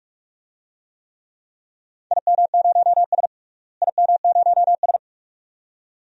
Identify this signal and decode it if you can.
Morse code, heard as a keyed tone, about 45 wpm, 700 Hz. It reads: IM0S IM0S
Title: DXCC-100 Call Signs - 4 Characters - Repeated 2x - 45wpm